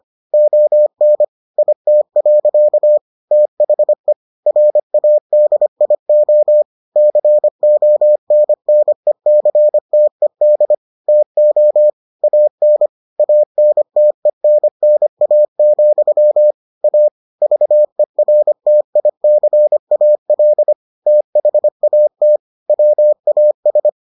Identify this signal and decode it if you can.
Morse code, heard as a keyed tone, about 25 wpm, 600 Hz. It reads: ON IT. THE RADIO CONNECTED TO AN ANTENNA, A VERTICAL THAT WAS